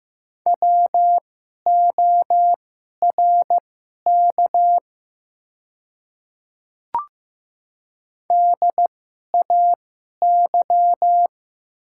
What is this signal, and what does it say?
Morse code, heard as a keyed tone, about 15 wpm, spaced as for 11 wpm, 700 Hz. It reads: WORK DAY
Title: 2x Character Spacing - Top 200 Words - 15wpm